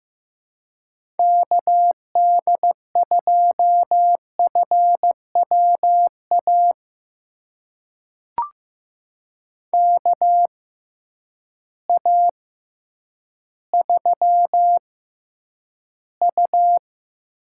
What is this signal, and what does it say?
Morse code, heard as a keyed tone, about 15 wpm, spaced as for 9 wpm, 700 Hz. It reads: KD2FWA KA3U
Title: USA Call Signs - 6 Characters - 6x Character Spacing - 15wpm